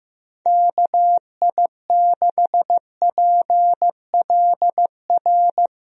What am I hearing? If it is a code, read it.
Morse code, heard as a keyed tone, about 16 wpm, 700 Hz. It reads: KI6PLR
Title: USA Call Signs - 6 Characters - Repeated 2x - 15wpm